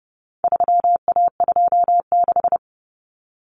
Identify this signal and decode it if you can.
Morse code, heard as a keyed tone, about 30 wpm, 700 Hz. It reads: 3A26
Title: Common Cut Numbers - 4-Digit 30wpm